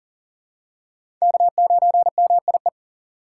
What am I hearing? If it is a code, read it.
Morse code, heard as a keyed tone, about 40 wpm, 700 Hz. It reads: K9MIE
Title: USA Call Signs - 4x Character Spacing - 40wpm